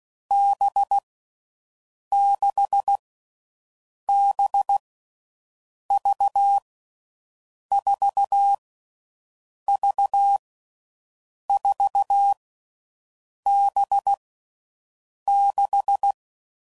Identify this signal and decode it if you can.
Morse code, heard as a keyed tone, about 16 wpm, 800 Hz. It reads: B 6 B V 4 V 4 B 6
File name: Groups / Basic Letter Groups / VB46